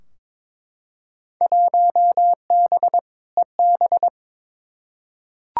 Morse code, heard as a keyed tone, about 22 wpm, 700 Hz. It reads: 1B EB
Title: Field Day Exchanges - 22wpm